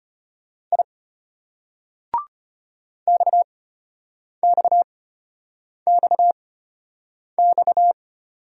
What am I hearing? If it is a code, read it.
Morse code, heard as a keyed tone, about 34 wpm, spaced as for 8 wpm, 700 Hz. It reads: I XXXX